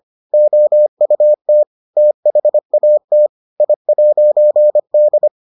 Morse code, heard as a keyed tone, about 25 wpm, 600 Hz. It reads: OUT THAT I'D